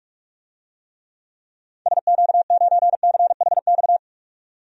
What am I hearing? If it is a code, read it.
Morse code, heard as a keyed tone, about 43 wpm, 700 Hz. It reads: SQ9CHX
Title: DXCC-100 Call Signs - 6 Characters - 45wpm